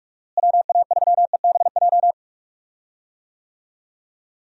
Morse code, heard as a keyed tone, about 41 wpm, 700 Hz. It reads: WA3EBJ
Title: USA Call Signs - 6 Characters - Repeated 3x - 45wpm